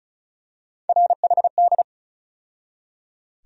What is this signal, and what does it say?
Morse code, heard as a keyed tone, about 36 wpm, 700 Hz. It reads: RHD
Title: USA Call Sign Postfixes - 3 Letters 35wpm